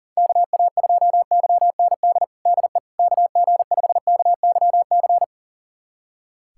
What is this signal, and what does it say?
Morse code, heard as a keyed tone, about 37 wpm, 700 Hz. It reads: KA2YND DE KG5KYC
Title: USA Callsign DE Callsign - 6 Characters - 40wpm